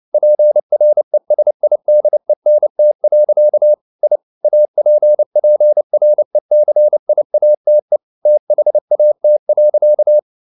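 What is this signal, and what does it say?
Morse code, heard as a keyed tone, about 29 wpm, 600 Hz. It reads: PRESIDENT. I APPRECIATE THAT.
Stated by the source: Morse code at 29 wpm, sending Ken Follett's novel Never in English, from CW med Gnister.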